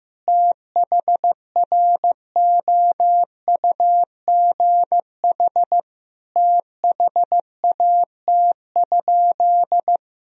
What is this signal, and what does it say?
Morse code, heard as a keyed tone, about 15 wpm, 700 Hz. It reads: THROUGH THAT?
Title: Sentences from Top 1000 Words - Review Part 1 - 15wpm